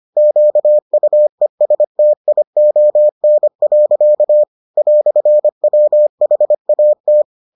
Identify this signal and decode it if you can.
Morse code, heard as a keyed tone, about 25 wpm, 600 Hz. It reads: QUESTION. "WHAT